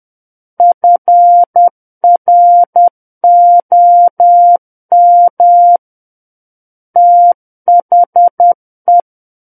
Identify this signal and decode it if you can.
Morse code, heard as a keyed tone, about 10 wpm, 700 Hz. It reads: FROM THE
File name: Book One / 01-Chapter 1 - 10 WPM